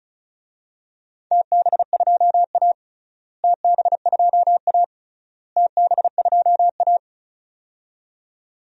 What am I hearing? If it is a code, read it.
Morse code, heard as a keyed tone, about 36 wpm, 700 Hz. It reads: TB2A TB2A TB2A
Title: DXCC-100 Call Signs - 4 Characters - Repeated 3x - 35wpm